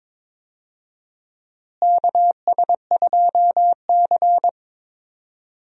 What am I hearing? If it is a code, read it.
Morse code, heard as a keyed tone, about 22 wpm, 700 Hz. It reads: KS2C
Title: USA Call Signs - 4 Characters - Repeated 2x - 22wpm